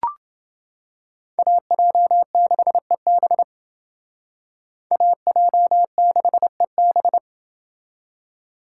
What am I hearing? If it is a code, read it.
Morse code, heard as a keyed tone, about 29 wpm, 700 Hz. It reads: AJ6EB AJ6EB